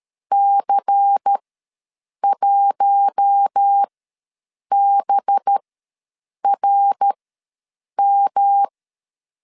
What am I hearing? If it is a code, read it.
Morse code, heard as a keyed tone, about 13 wpm, spaced as for 7 wpm, 800 Hz. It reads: C1BRM
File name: LEZIONI / 5 LBVC / 5 LBVC ws 12wpm 3159